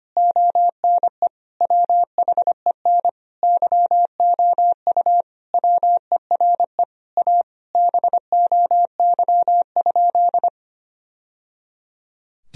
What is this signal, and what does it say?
Morse code, heard as a keyed tone, about 25 wpm, 700 Hz. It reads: ONE WHEN YOU WERE A BOY?
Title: Sentences from Top 1000 Words - Review Part 1 - 25wpm